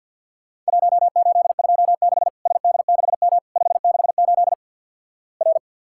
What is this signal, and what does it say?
Morse code, heard as a keyed tone, about 53 wpm, 700 Hz. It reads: 1826 SD6M 568 R
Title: Worked All Europe DX Contest - QTC Exchange - 50wpm